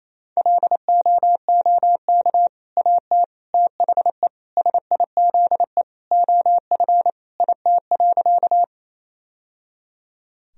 Morse code, heard as a keyed tone, about 27 wpm, 700 Hz. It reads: LOOK AT THE SIZE OF IT.
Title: Sentences from Top 1000 Words - Review Part 2 - 28wpm